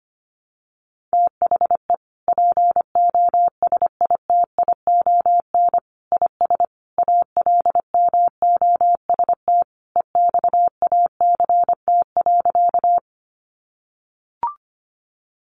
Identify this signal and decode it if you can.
Morse code, heard as a keyed tone, about 25 wpm, 700 Hz. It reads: THE POSITION IS ALMOST EXACT.
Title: Sentences from Top 1300 Words 25wpm